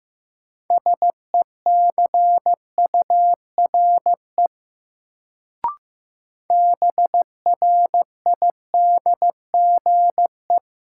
Morse code, heard as a keyed tone, about 15 wpm, 700 Hz. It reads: SECURE BRIDGE
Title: Sentences from Top 1200 Words - New Word Drill Alt - 15wpm